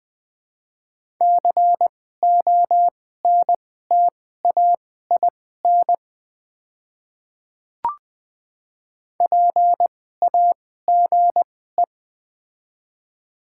Morse code, heard as a keyed tone, about 21 wpm, spaced as for 15 wpm, 690 Hz. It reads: CONTAIN PAGE
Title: 2x Character Spacing - Top 500 Words - 20wpm